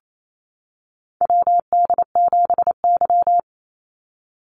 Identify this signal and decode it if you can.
Morse code, heard as a keyed tone, about 28 wpm, 700 Hz. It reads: WD7Y
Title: NA SOTA Chaser Callsigns 28wpm